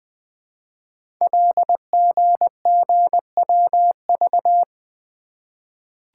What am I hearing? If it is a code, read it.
Morse code, heard as a keyed tone, about 20 wpm, 700 Hz. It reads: LGGWV
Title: Code Groups - 5 Letters - 20wpm